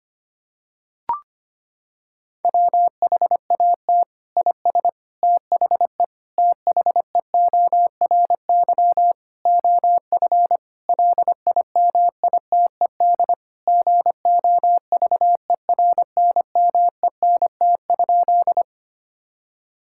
Morse code, heard as a keyed tone, about 25 wpm, 700 Hz. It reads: WHAT IS THE THEORY OF LIMITED GOVERNMENT?
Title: Sentences from Top 1000 Words - Review Part 6 - 25wpm